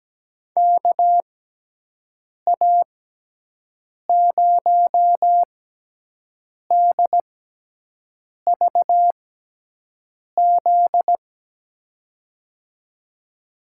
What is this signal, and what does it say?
Morse code, heard as a keyed tone, about 17 wpm, spaced as for 6 wpm, 700 Hz. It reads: KA0DVZ
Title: USA Call Signs - 6 Characters - 6x Character Spacing - 17wpm